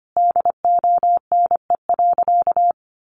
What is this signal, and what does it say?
Morse code, heard as a keyed tone, about 25 wpm, 700 Hz. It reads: DONE.
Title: Sentences from Top 700 Words 25wpm